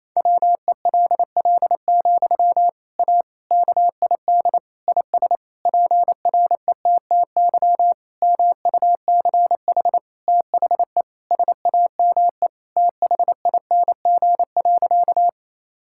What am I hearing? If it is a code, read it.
Morse code, heard as a keyed tone, about 28 wpm, 700 Hz. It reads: WELL, A KID IS PRETTY MUCH THE SAME THING.
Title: Sentences from Top 1600 Words 28wpm